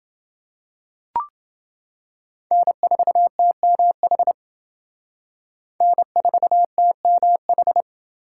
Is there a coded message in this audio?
Yes: Morse code from 2.5 to 7.8 s, heard as a keyed tone, about 28 wpm, 700 Hz. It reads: N4TMH N4TMH